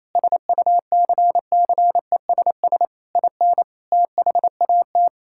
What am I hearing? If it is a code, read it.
Morse code, heard as a keyed tone, about 28 wpm, 700 Hz. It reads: SUCCESS IN THAT
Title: Sentences from Top 1000 Words - Review Part 5 - 28wpm